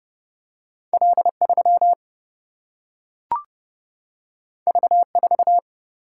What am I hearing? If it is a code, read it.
Morse code, heard as a keyed tone, about 30 wpm, 700 Hz. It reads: L 3 V 4
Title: Single-Letter Single-Number 30wpm